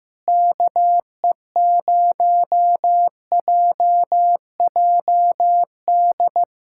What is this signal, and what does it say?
Morse code, heard as a keyed tone, about 15 wpm, 700 Hz. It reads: KE0JJD
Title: USA Callsign DE Callsign 15wpm